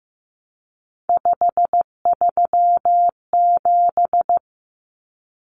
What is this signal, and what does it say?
Morse code, heard as a keyed tone, about 15 wpm, 700 Hz. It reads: 537
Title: Numbers - 3-Digit 15wpm